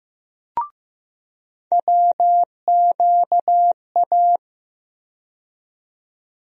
Morse code, heard as a keyed tone, about 15 wpm, 700 Hz. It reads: WQA
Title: USA Call Sign Postfixes - 3 Letters 15wpm